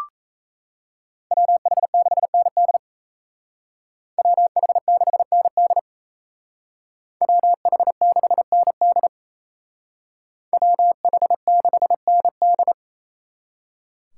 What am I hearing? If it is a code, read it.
Morse code, heard as a keyed tone, about 34 wpm, 700 Hz. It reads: WH6ND WH6ND WH6ND WH6ND
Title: Speed Racing - USA Call Signs 28wpm